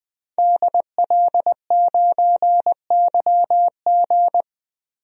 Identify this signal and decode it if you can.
Morse code, heard as a keyed tone, about 20 wpm, 700 Hz. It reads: DL9YG
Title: DXCC-100 Call Signs - 5 Characters - Repeated 2x - 20wpm